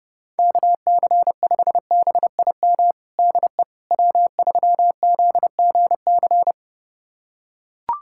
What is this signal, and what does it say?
Morse code, heard as a keyed tone, about 30 wpm, 700 Hz. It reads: KC5BIM DE W3ZGC
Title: USA Callsign DE Callsign 30wpm